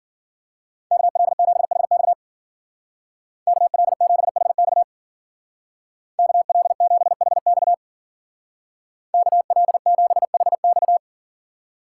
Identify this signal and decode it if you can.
Morse code, heard as a keyed tone, about 52 wpm, 700 Hz. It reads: KL7HX KL7HX KL7HX KL7HX